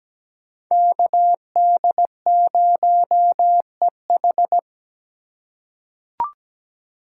Morse code, heard as a keyed tone, about 17 wpm, 700 Hz. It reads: KD0EH
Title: USA Call Signs - 3x Character Spacing - 17wpm